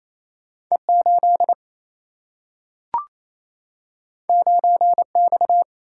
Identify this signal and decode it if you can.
Morse code, heard as a keyed tone, about 28 wpm, 700 Hz. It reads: E8 9X
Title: Single-Letter Single-Number 28wpm